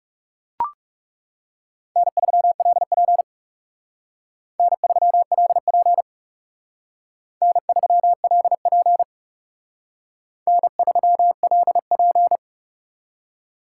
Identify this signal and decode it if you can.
Morse code, heard as a keyed tone, about 37 wpm, 700 Hz. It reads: N3LP N3LP N3LP N3LP
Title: Speed Racing - USA Call Signs 30wpm